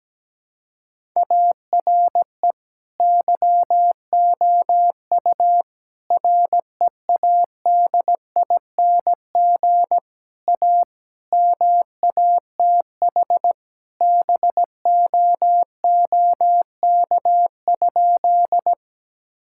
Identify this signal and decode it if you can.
Morse code, heard as a keyed tone, about 17 wpm, 700 Hz. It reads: ARE YOU READING A MATH BOOK?
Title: Sentences from Top 1000 Words - Review Part 6 - 17wpm